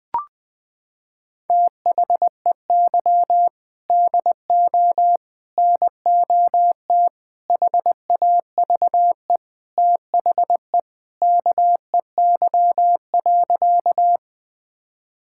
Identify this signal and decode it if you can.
Morse code, heard as a keyed tone, about 20 wpm, 700 Hz. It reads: THEY DO NOT HAVE THE KEY.
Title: Sentences with 5-Word Phrases - 20wpm